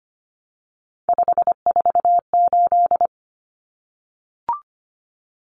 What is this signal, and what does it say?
Morse code, heard as a keyed tone, about 25 wpm, 700 Hz. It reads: 548
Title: Numbers - 3-Digit 25wpm